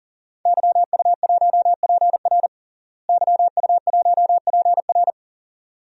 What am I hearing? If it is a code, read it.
Morse code, heard as a keyed tone, about 40 wpm, 700 Hz. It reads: YU1PR YU1PR
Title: DXCC-100 Call Signs - 5 Characters - Repeated 2x - 40wpm